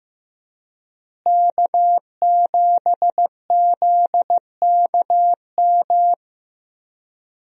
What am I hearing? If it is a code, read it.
Morse code, heard as a keyed tone, about 15 wpm, 700 Hz. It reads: K7ZKM
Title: USA Call Signs - 5 Characters - Repeated 2x - 15wpm